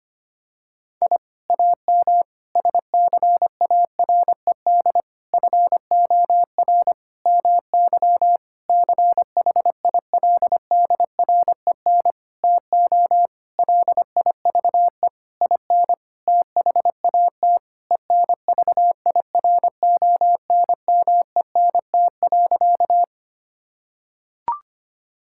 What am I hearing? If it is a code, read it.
Morse code, heard as a keyed tone, about 25 wpm, 700 Hz. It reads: I AM SCARED FOR MY CHILDREN TO LIVE IN THAT ENVIRONMENT.
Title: Sentences from Top 1000 Words - Review Part 5 - 25wpm